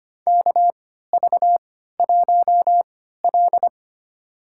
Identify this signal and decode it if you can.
Morse code, heard as a keyed tone, about 25 wpm, spaced as for 14 wpm, 700 Hz. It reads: KV1L
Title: USA Call Signs - 4 Characters - 3x Character Spacing - 25wpm